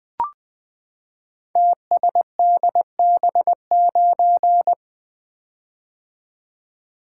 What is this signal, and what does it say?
Morse code, heard as a keyed tone, about 20 wpm, 700 Hz. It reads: TSDB9
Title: Code Groups - 5 Letter:Numbers - 20wpm